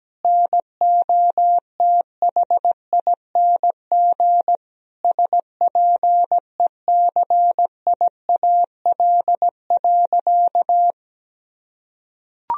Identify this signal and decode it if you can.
Morse code, heard as a keyed tone, about 17 wpm, 700 Hz. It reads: NOTHING SPECIAL.
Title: Sentences from Top 1000 Words - Review Part 3 - 17wpm